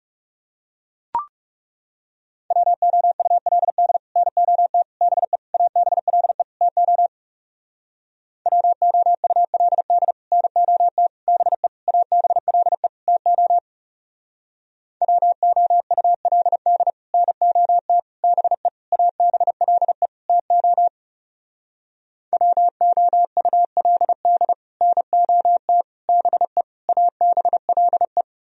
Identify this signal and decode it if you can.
Morse code, heard as a keyed tone, about 35 wpm, 700 Hz. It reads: WOULD NOT BE ABLE TO WOULD NOT BE ABLE TO WOULD NOT BE ABLE TO WOULD NOT BE ABLE
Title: Speed-Racing - Sets of 5 Words 30wpm